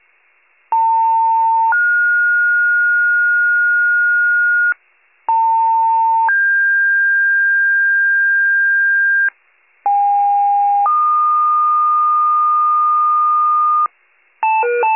Two-tone page, heard as a one-second tone, then a three-second tone, about 900 then 1430 Hz, about 900 then 1600 Hz, about 810 then 1220 Hz, with noise. It is heard from 0.7 to 13.9 s.
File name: wcalert